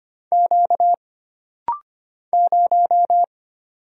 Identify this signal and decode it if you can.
Morse code, heard as a keyed tone, about 25 wpm, 700 Hz. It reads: Q 0
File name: Single Character (Letter:Number:Punctuation:Prosign) - Rapid-Fire - 25wpm